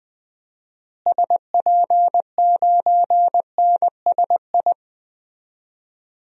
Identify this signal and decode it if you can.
Morse code, heard as a keyed tone, about 20 wpm, 700 Hz. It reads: SP9NSI
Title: DXCC-100 Call Signs - 6 Characters - Repeated 2x - 20wpm